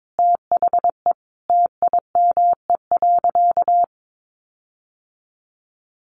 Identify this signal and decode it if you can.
Morse code, heard as a keyed tone, about 22 wpm, 700 Hz. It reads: THE TIME.
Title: Sentences from Top 1400 Words 22wpm